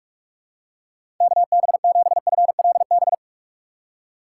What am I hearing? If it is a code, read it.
Morse code, heard as a keyed tone, about 44 wpm, 700 Hz. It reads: KB7FLB
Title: USA Call Signs - 6 Characters - 6x Character Spacing - 45wpm